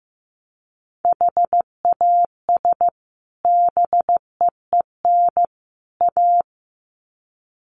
Morse code, heard as a keyed tone, about 15 wpm, 700 Hz. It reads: HAS BEEN A
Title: Sets of 3 Words 15wpm